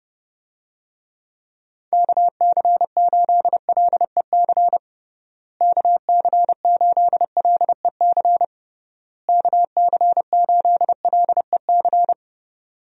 Morse code, heard as a keyed tone, about 30 wpm, 700 Hz. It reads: KC8LEC KC8LEC KC8LEC
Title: USA Call Signs - 6 Characters - Repeated 3x - 30wpm